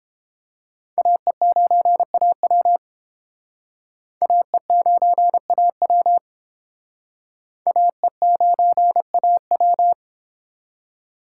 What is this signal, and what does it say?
Morse code, heard as a keyed tone, about 31 wpm, 700 Hz. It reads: AE9AW AE9AW AE9AW